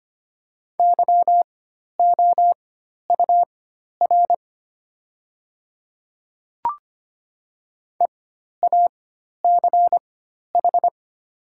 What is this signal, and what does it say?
Morse code, heard as a keyed tone, about 25 wpm, spaced as for 12 wpm, 700 Hz. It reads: YOUR EACH
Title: 4x Character Spacing - Top 100 Words - 25wpm